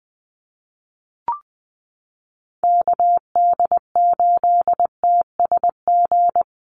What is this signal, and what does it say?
Morse code, heard as a keyed tone, about 20 wpm, 700 Hz. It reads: KD8TSG
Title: USA Call Signs 20wpm